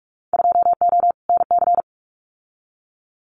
Morse code, heard as a keyed tone, about 46 wpm, 700 Hz. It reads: 2O NC
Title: Winter Field Day Exchanges 45wpm